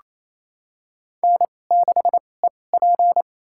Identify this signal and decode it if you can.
Morse code, heard as a keyed tone, about 28 wpm, spaced as for 20 wpm, 700 Hz. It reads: N6EP
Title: USA Call Signs - 2x Character Spacing - 28wpm